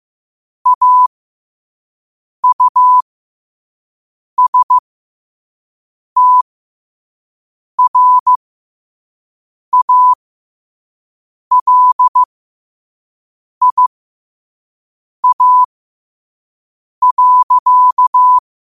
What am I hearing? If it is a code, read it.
Morse code, heard as a keyed tone, about 15 wpm, spaced as for 5 wpm, 1000 Hz. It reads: AUSTRALIA.